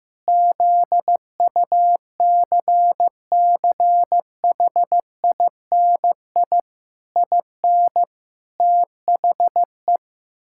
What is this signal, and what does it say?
Morse code, heard as a keyed tone, about 16 wpm, 700 Hz. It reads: ZUCCHINI IN THE
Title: Pangram Challenge v2 - 15wpm